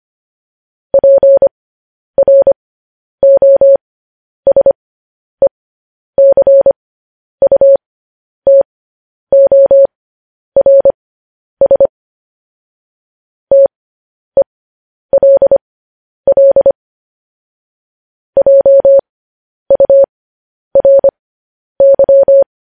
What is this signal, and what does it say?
Morse code, heard as a keyed tone, about 25 wpm, spaced as for 10 wpm, 550 Hz. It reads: PROSECUTORS TELL JURY
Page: News Headlines in Morse Code at 10 WPM